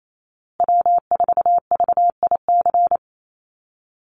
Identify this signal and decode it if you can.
Morse code, heard as a keyed tone, about 28 wpm, 700 Hz. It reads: W4VIC
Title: Call Signs Repeated 2x - 28wpm